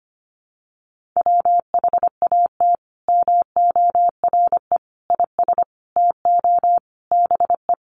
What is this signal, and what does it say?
Morse code, heard as a keyed tone, about 25 wpm, 700 Hz. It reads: WHAT MORE IS TO BE
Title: Sentences from Top 500 Words 25wpm